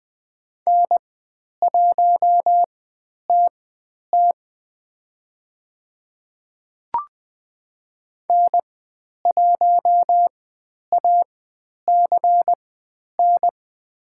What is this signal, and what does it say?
Morse code, heard as a keyed tone, about 21 wpm, spaced as for 10 wpm, 700 Hz. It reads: N1TT N1ACN
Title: 109 Lesson 18g - Letter m - Call Signs 20wpm